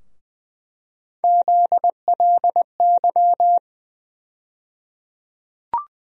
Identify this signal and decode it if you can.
Morse code, heard as a keyed tone, about 20 wpm, 700 Hz. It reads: ZLY
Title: USA Call Sign Postfixes - 3 Letters 20wpm